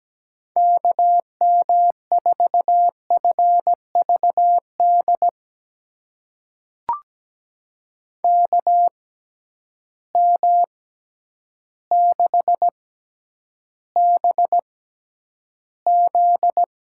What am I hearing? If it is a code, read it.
Morse code, heard as a keyed tone, about 17 wpm, spaced as for 9 wpm, 700 Hz. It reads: KM4FVD KM6BZ
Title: USA Call Signs - 6 Characters - 6x Character Spacing - 17wpm